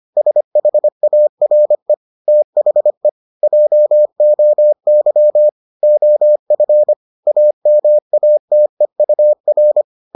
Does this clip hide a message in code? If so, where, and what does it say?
Yes: Morse code from 0.2 to 9.8 s, heard as a keyed tone, about 25 wpm, 600 Hz. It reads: SHARE THE JOY OF AMATEUR